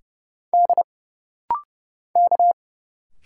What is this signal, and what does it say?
Morse code, heard as a keyed tone, about 30 wpm, 700 Hz. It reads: D K
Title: Single Letters - Rapid-Fire 30wpm